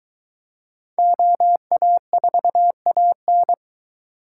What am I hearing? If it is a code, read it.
Morse code, heard as a keyed tone, about 22 wpm, 700 Hz. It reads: OA4AN